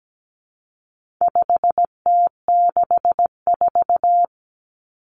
Morse code, heard as a keyed tone, about 17 wpm, 700 Hz. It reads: 5T64
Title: Common Cut Numbers - 4-Digit 17wpm